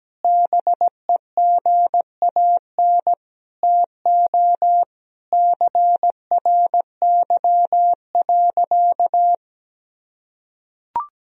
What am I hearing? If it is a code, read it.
Morse code, heard as a keyed tone, about 17 wpm, 700 Hz. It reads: BEGAN TO CRY.
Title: Sentences from Top 1000 Words - Review Part 2 - 17wpm